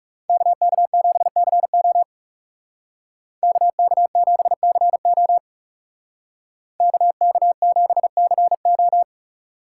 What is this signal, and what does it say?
Morse code, heard as a keyed tone, about 37 wpm, 700 Hz. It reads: KK7CO KK7CO KK7CO
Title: Speed Racing - USA 2x2 Call Signs 30wpm